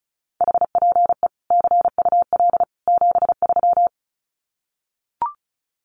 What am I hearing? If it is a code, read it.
Morse code, heard as a keyed tone, about 34 wpm, 700 Hz. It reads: HPE CUL 73
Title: Common QSO Phrases 35wpm